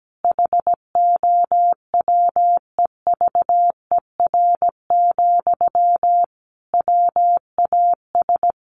Morse code, heard as a keyed tone, about 17 wpm, 700 Hz. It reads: HOWEVER, WAS
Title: Sentences from Top 1400 Words 17wpm